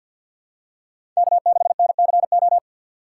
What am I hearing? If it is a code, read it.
Morse code, heard as a keyed tone, about 53 wpm, 700 Hz. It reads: K6NCO
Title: USA Call Signs - 5 Characters - 4x Character Spacing - 50wpm